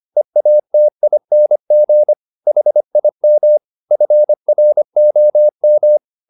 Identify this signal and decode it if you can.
Morse code, heard as a keyed tone, about 25 wpm, 600 Hz. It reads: EATING HIM FROM